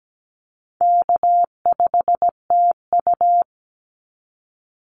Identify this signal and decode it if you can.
Morse code, heard as a keyed tone, about 17 wpm, 700 Hz. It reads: K5TU
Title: Easy Call Signs Repeated 2x - 17wpm